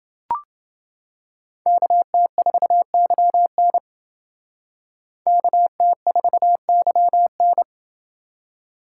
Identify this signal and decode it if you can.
Morse code, heard as a keyed tone, about 28 wpm, 700 Hz. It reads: KT4YN KT4YN